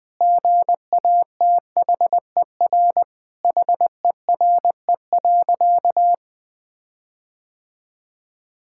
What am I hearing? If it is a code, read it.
Morse code, heard as a keyed tone, about 20 wpm, 700 Hz. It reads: GATHER HERE.
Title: Sentences with 6-Word Phrases - Spoken Beginning - 20wpm